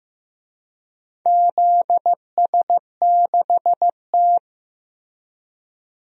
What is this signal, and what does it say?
Morse code, heard as a keyed tone, about 15 wpm, 700 Hz. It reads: ZS6T
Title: DXCC-100 Call Signs - 4 Characters - Repeated 3x - 15wpm